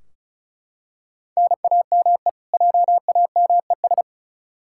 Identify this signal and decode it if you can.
Morse code, heard as a keyed tone, about 36 wpm, 700 Hz. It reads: NAME JAMES
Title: Practice QSO - Exchange 1 - Beginning - 35wpm